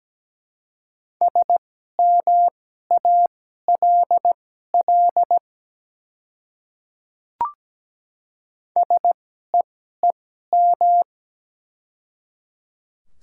Morse code, heard as a keyed tone, about 17 wpm, spaced as for 12 wpm, 690 Hz. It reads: SMALL SEEM
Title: 2x Character Spacing - Top 300 Words - 17wpm